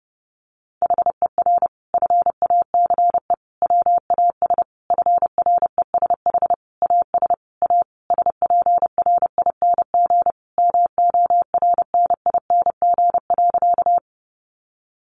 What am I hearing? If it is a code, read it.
Morse code, heard as a keyed tone, about 30 wpm, 700 Hz. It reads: HER FACE WAS FRESH AS A SPRING MORNING.
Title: Sentences from Top 1200 Words 30wpm